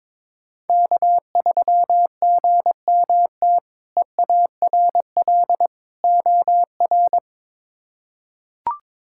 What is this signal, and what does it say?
Morse code, heard as a keyed tone, about 22 wpm, 700 Hz. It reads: K3GMT EARL OR
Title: K1USN SST QSO Practice - 22wpm